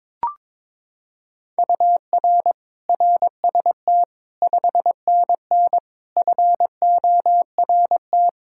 Morse code, heard as a keyed tone, about 22 wpm, 700 Hz. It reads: UR RST 5NN FORT
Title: Practice QSO - Exchange 1 - Beginning - 22wpm